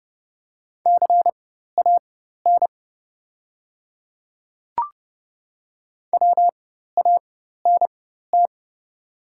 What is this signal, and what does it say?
Morse code, heard as a keyed tone, about 30 wpm, spaced as for 14 wpm, 700 Hz. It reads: CAN WANT
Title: 4x Character Spacing - Top 300 Words - 30wpm